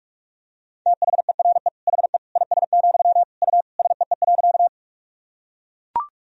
Morse code, heard as a keyed tone, about 41 wpm, 700 Hz. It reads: THERE HE IS, U SEE.
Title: Sentences from Top 100 Words and Abbreviations 45wpm